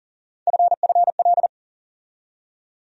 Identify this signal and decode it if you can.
Morse code, heard as a keyed tone, about 40 wpm, 700 Hz. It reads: FFL
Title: USA Call Sign Postfixes - 3 Letters 40wpm